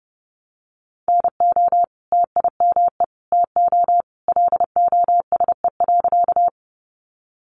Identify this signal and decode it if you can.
Morse code, heard as a keyed tone, about 30 wpm, 700 Hz. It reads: NO TIME TO LOSE.
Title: Sentences from Top 1100 Words 30wpm